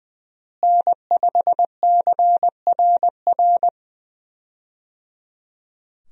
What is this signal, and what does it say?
Morse code, heard as a keyed tone, about 20 wpm, 700 Hz. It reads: N5CRR
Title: USA Call Signs - 5 Characters - Repeated 3x - 20wpm